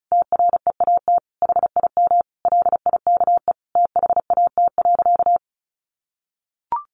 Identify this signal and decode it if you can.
Morse code, heard as a keyed tone, about 34 wpm, 700 Hz. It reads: TREAT HIM LIKE THAT.
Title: Sentences from Top 1300 Words 35wpm